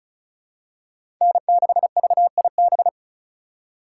Morse code, heard as a keyed tone, about 36 wpm, 700 Hz. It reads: N6VIB
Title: USA Call Signs - 5 Characters - 4x Character Spacing - 35wpm